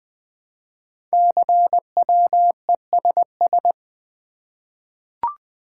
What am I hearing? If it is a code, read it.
Morse code, heard as a keyed tone, about 20 wpm, 700 Hz. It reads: CWESS
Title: Code Groups - 5 Letters - 20wpm